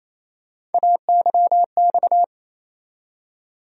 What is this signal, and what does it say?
Morse code, heard as a keyed tone, about 28 wpm, 700 Hz. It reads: AYX
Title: USA Call Sign Postfixes - 3 Letters 28wpm